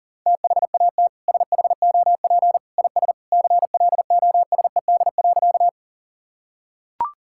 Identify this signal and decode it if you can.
Morse code, heard as a keyed tone, about 40 wpm, 700 Hz. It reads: THAT SHOP IS CLOSED.
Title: Sentences from Top 1600 Words 40wpm